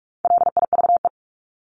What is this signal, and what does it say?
Morse code, heard as a keyed tone, about 44 wpm, 700 Hz. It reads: LIVE
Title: Top 500 Words 45wpm